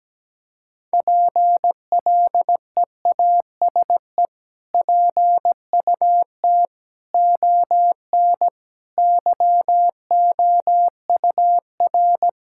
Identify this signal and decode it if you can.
Morse code, heard as a keyed tone, about 17 wpm, 700 Hz. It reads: PLEASE PUT ON YOUR
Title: Sentences from Top 1600 Words 17wpm